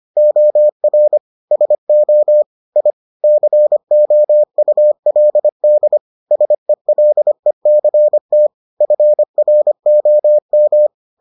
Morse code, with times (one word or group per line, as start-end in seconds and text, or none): 0.2-1.2 OR
1.5-2.4 SO
2.8-2.9 I
3.2-6.0 COULD
6.3-8.5 SELECT
8.8-10.9 FROM